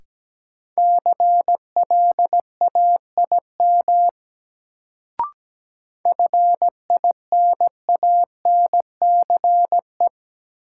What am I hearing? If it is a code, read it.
Morse code, heard as a keyed tone, about 17 wpm, 700 Hz. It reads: CLAIM FINANCE
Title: Sentences from Top 1500 Words - New Word Drill Alt - 17wpm